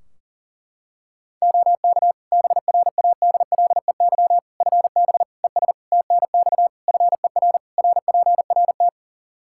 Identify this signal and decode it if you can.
Morse code, heard as a keyed tone, about 40 wpm, 700 Hz. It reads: OK BRADLEY FB ES TNX FER RPRT
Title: Practice QSO - Exchange 2 - Info - 40wpm